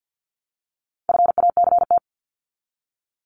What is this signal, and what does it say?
Morse code, heard as a keyed tone, about 48 wpm, 700 Hz. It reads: FACT
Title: Top 500 Words 50wpm